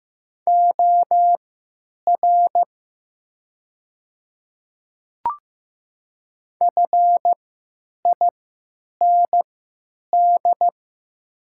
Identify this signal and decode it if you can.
Morse code, heard as a keyed tone, about 15 wpm, spaced as for 9 wpm, 700 Hz. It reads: OR FIND
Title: 3x Character Spacing - Top 500 Words - 15wpm